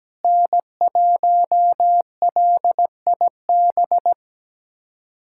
Morse code, heard as a keyed tone, about 17 wpm, 700 Hz. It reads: N1LIB
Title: USA Callsign DE Callsign - 5 Characters - 17wpm